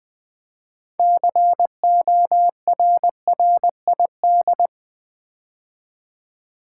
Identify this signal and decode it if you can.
Morse code, heard as a keyed tone, about 20 wpm, 700 Hz. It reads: CORRID
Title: Build A Word - 2 Letters At A Time - 8-Letter Words - 20wpm